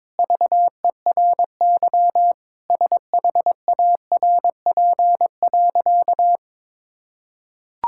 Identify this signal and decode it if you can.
Morse code, heard as a keyed tone, about 22 wpm, 700 Hz. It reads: VERY SHARP.
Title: Sentences from Top 1600 Words 22wpm